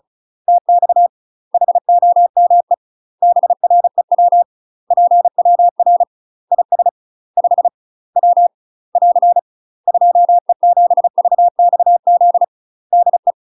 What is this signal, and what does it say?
Morse code, heard as a keyed tone, about 35 wpm, 700 Hz. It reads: TX HOME BREW PWR IS 5 W <AR> 2E7VXZ DE
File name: UK MASTER QSOs TRAINING / 35WPM / UK MASTER QSOs 0038